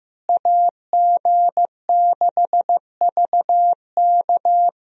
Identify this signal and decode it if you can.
Morse code, heard as a keyed tone, about 15 wpm, 700 Hz. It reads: AG6VK
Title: USA Call Signs - 5 Characters - Repeated 3x - 15wpm